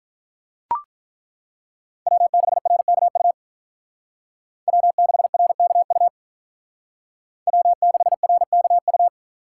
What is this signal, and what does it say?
Morse code, heard as a keyed tone, about 42 wpm, 700 Hz. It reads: W6RKU W6RKU W6RKU
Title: Speed Racing - USA 1x3 Call Signs 35wpm